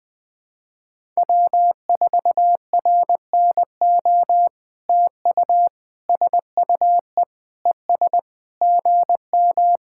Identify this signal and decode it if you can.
Morse code, heard as a keyed tone, about 20 wpm, 700 Hz. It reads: W4RNO TU SUE ES GM
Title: POTA QSO Practice - 20wpm